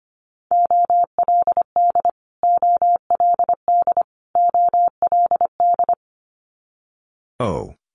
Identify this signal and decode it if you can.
Morse code, heard as a keyed tone, about 25 wpm, 700 Hz. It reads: OLD OLD OLD
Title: Multi-letter Phonemes Repeated 3x 25wpm